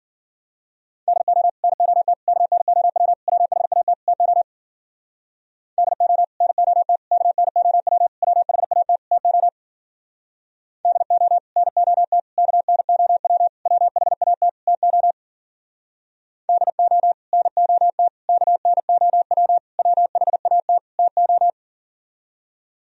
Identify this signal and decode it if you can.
Morse code, heard as a keyed tone, about 52 wpm, 700 Hz. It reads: DO NOT KNOW WHAT TO DO NOT KNOW WHAT TO DO NOT KNOW WHAT TO DO NOT KNOW WHAT TO